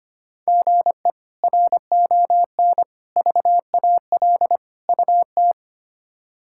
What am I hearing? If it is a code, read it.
Morse code, heard as a keyed tone, about 25 wpm, 700 Hz. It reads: GE RON VAL UT
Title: K1USN SST QSO Practice - 25wpm